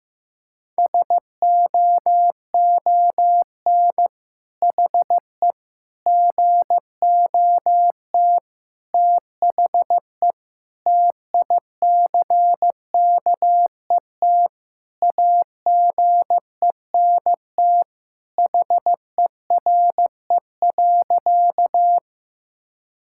Morse code, heard as a keyed tone, about 16 wpm, 700 Hz. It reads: SOON HE GOT THE TICKET AGENT HERE.
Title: Sentences from Top 1600 Words 15wpm